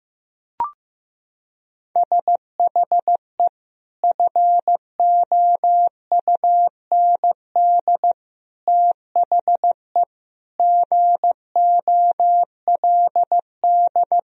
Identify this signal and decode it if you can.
Morse code, heard as a keyed tone, about 15 wpm, 700 Hz. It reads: SHE FOUND THE GOLD
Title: Sentences from Top 1000 Words - Review Part 3 - 15wpm